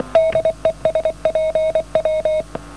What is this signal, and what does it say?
Morse code, heard as a keyed tone, about 27 wpm, 590 Hz. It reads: DE SPW